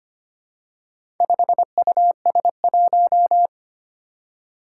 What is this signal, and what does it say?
Morse code, heard as a keyed tone, about 25 wpm, 700 Hz. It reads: 5US1
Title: Four Challenging Letter-Number Combinations 25wpm